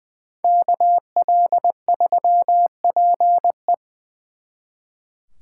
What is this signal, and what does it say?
Morse code, heard as a keyed tone, about 25 wpm, 700 Hz. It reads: KL3PE
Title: Speed Racing - USA 2x2 Call Signs 20wpm